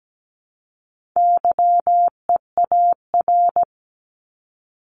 Morse code, heard as a keyed tone, about 17 wpm, 700 Hz. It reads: YEAR
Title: ES:S Suffix - Top 100 Words with Base Words 17wpm